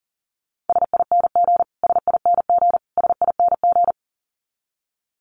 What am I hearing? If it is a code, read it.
Morse code, heard as a keyed tone, about 40 wpm, 700 Hz. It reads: SING SING SING
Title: Top 500 Words Repeated 40wpm